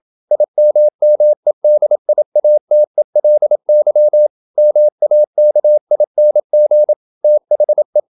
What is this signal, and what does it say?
Morse code, heard as a keyed tone, about 27 wpm, 600 Hz. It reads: IMMEDIATELY MAKING THE
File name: Never 0027wpm